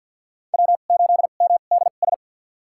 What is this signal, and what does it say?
Morse code, heard as a keyed tone, about 50 wpm, spaced as for 36 wpm, 700 Hz. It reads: W8MDS
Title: USA Call Signs - 2x Character Spacing - 50wpm